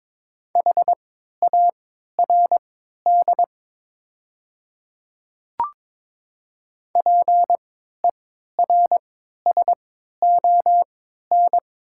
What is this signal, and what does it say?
Morse code, heard as a keyed tone, about 22 wpm, spaced as for 12 wpm, 700 Hz. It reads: HARD PERSON
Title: 3x Character Spacing - Top 500 Words - 22wpm